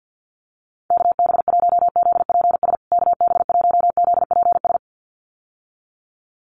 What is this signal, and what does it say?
Morse code, heard as a keyed tone, about 50 wpm, 700 Hz. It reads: KB1ZPS KB1ZPS
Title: USA Call Signs Repeated 2x 50wpm